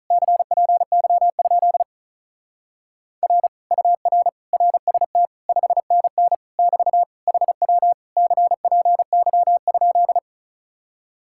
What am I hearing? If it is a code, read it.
Morse code, heard as a keyed tone, about 42 wpm, 700 Hz. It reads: CPY? R UR RST 5NN <BT> HW CPY?
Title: Speed-Racing Common QSO Phrases - 35wpm